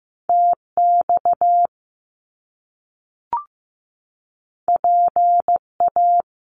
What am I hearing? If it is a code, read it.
Morse code, heard as a keyed tone, about 15 wpm, 700 Hz. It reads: TX PA
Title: US States 15wpm